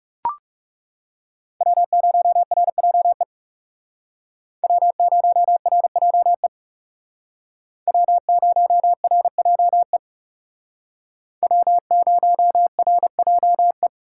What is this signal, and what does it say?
Morse code, heard as a keyed tone, about 37 wpm, 700 Hz. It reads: W0RJE W0RJE W0RJE W0RJE
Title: Speed Racing - USA Call Signs 30wpm